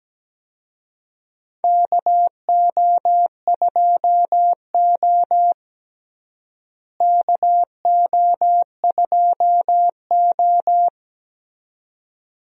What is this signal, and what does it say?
Morse code, heard as a keyed tone, about 17 wpm, 700 Hz. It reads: KO2O KO2O
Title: USA Call Signs - 4 Characters - Repeated 2x - 17wpm